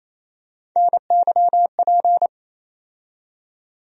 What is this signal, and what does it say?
Morse code, heard as a keyed tone, about 28 wpm, 700 Hz. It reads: NYP
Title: USA Call Sign Postfixes - 3 Letters 28wpm